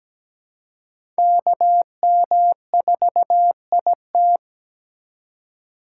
Morse code, heard as a keyed tone, about 17 wpm, 700 Hz. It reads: KM4IT
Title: USA Call Signs - 5 Characters - 2x Character Spacing - 17wpm